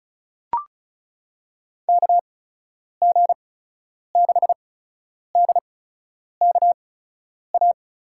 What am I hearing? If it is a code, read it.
Morse code, heard as a keyed tone, about 34 wpm, spaced as for 10 wpm, 700 Hz. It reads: KG6DKA
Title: USA Call Signs - 6 Characters - 8x Character Spacing - 35wpm